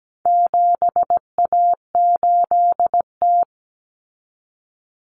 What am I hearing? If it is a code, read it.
Morse code, heard as a keyed tone, about 17 wpm, 700 Hz. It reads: 7A8T
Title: Common Cut Numbers - 4-Digit 17wpm